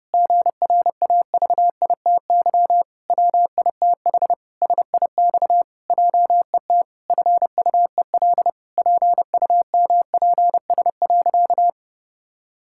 Morse code, heard as a keyed tone, about 32 wpm, 700 Hz. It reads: GRAVITY WITH SIX JET FUEL PUMPS.
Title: Pangram Challenge v2 - 30wpm